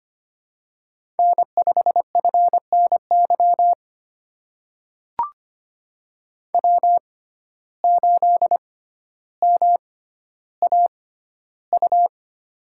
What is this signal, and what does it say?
Morse code, heard as a keyed tone, about 25 wpm, spaced as for 13 wpm, 700 Hz. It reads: N5FNY W8MAU
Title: USA Call Signs - 5 Characters - 6x Character Spacing - 25wpm